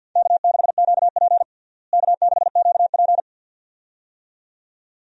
Morse code, heard as a keyed tone, about 48 wpm, 700 Hz. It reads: K6QP K6QP
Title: USA Call Signs - 4 Characters - Repeated 2x - 50wpm